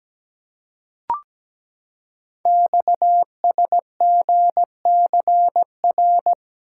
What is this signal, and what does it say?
Morse code, heard as a keyed tone, about 17 wpm, 700 Hz. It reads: XSGCR
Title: Code Groups - 5 Letters - 17wpm